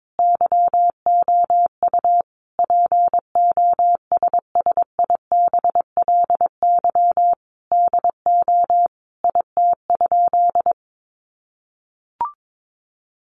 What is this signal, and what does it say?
Morse code, heard as a keyed tone, about 22 wpm, 700 Hz. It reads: YOU POSSIBLY DO IT?
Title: Sentences from Top 1300 Words 22wpm